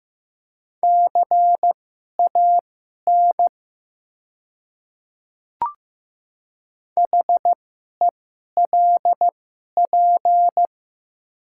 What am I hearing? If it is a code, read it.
Morse code, heard as a keyed tone, about 15 wpm, spaced as for 11 wpm, 690 Hz. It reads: CAN HELP
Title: 2x Character Spacing - Top 200 Words - 15wpm